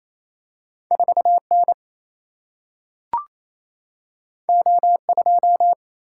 Morse code, heard as a keyed tone, about 28 wpm, 700 Hz. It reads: 4N O2
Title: Single-Letter Single-Number 28wpm